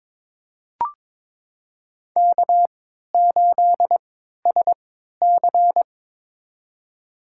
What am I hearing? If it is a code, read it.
Morse code, heard as a keyed tone, about 23 wpm, spaced as for 13 wpm, 690 Hz. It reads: K8SC
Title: USA Call Signs - 4 Characters - 3x Character Spacing - 22wpm